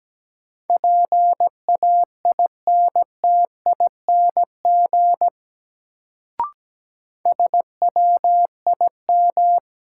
Morse code, heard as a keyed tone, about 17 wpm, 700 Hz. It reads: PAINTING SWIM
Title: Sentences from Top 1000 Words - Review Part 6 - New Word Drill - 17wpm